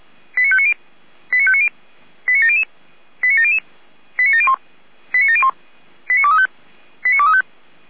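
Five-tone selective call (ZVEI), heard as quick runs of five tones, tones of about 70 ms. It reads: 89590 89590 89700 89700 89701 89701 89235 89235
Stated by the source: ZVEI five-tone selective call 89115 received by SDR.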